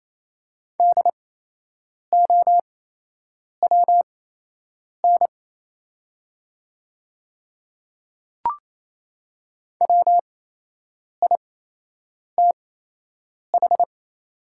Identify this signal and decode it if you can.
Morse code, heard as a keyed tone, about 28 wpm, spaced as for 8 wpm, 700 Hz. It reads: DOWN WITH
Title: 8x Character Spacing - Top 100 Words - 28wpm